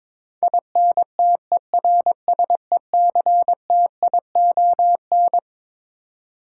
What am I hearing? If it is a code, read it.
Morse code, heard as a keyed tone, about 22 wpm, 700 Hz. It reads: INTERSECTION
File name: Twelve Letter Words - 22wpm